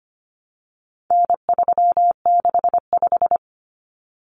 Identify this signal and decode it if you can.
Morse code, heard as a keyed tone, about 25 wpm, 700 Hz. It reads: N365
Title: Common Cut Numbers - 4-Digit 25wpm